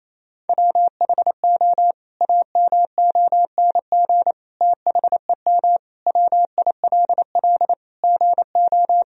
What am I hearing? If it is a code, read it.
Morse code, heard as a keyed tone, about 28 wpm, 700 Hz. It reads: WHO AMONG THEM WILL GO
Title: Sentences from Top 1000 Words - Review Part 3 - 28wpm